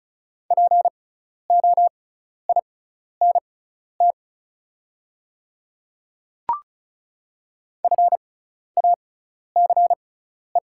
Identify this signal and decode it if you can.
Morse code, heard as a keyed tone, about 36 wpm, spaced as for 12 wpm, 700 Hz. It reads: POINT FACE
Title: 6x Character Spacing - Top 500 Words - 35wpm